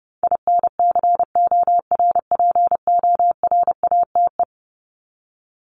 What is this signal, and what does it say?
Morse code, heard as a keyed tone, about 30 wpm, 700 Hz. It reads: INCORPORATE
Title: Top 1751-2000 Words 30wpm